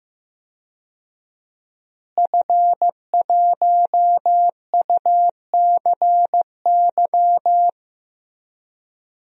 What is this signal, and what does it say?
Morse code, heard as a keyed tone, about 15 wpm, 700 Hz. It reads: F1UCY
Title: DXCC-100 Call Signs - 5 Characters - Repeated 3x - 15wpm